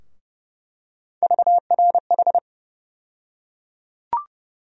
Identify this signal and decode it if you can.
Morse code, heard as a keyed tone, about 30 wpm, 700 Hz. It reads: VRH
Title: USA Call Sign Postfixes - 3 Letters 30wpm